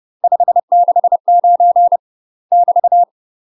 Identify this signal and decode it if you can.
Morse code, heard as a keyed tone, about 30 wpm, 700 Hz. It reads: 569 <BT>
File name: UK MASTER QSOs TRAINING / 30WPM / UK MASTER QSOs 0039